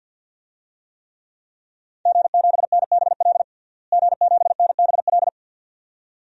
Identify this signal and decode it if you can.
Morse code, heard as a keyed tone, about 48 wpm, 700 Hz. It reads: G7NBL G7NBL
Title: DXCC-100 Call Signs - 5 Characters - Repeated 2x - 50wpm